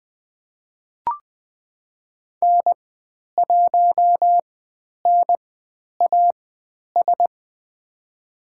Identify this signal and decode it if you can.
Morse code, heard as a keyed tone, about 20 wpm, spaced as for 10 wpm, 700 Hz. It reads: N1NAS
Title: 095 Lesson 16g - Letter c - Call Signs 20wpm